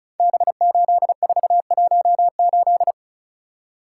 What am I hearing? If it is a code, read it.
Morse code, heard as a keyed tone, about 36 wpm, 700 Hz. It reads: B8418
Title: Code Groups - 5 Letter:Numbers - 35wpm